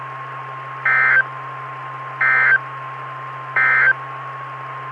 Emergency Alert System (SAME) tones, heard as harsh buzzing bursts. It is heard from 0.7 to 4.0 s.
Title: Saline, AK SKYWARN Repeater System - W5RHS